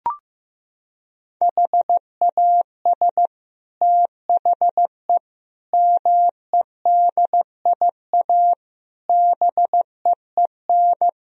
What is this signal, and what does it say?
Morse code, heard as a keyed tone, about 15 wpm, 700 Hz. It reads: HAS THE MEDIA BEEN
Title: Sentences from Top 1000 Words - Review Part 4 - 15wpm